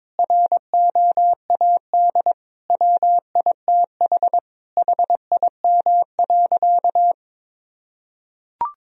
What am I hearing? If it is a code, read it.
Morse code, heard as a keyed tone, about 22 wpm, 700 Hz. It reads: ROAD WITH HIM.
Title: Sentences from Top 1000 Words - Review Part 2 - 22wpm